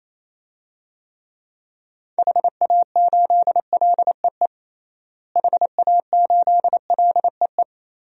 Morse code, heard as a keyed tone, about 27 wpm, 700 Hz. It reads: HA8LEE HA8LEE
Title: DXCC-100 Call Signs - 6 Characters - Repeated 2x - 28wpm